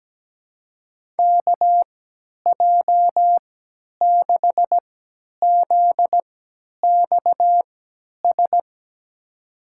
Morse code, heard as a keyed tone, about 17 wpm, spaced as for 10 wpm, 700 Hz. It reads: KJ6ZXS
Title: USA Call Signs - 6 Characters - 3x Character Spacing - 17wpm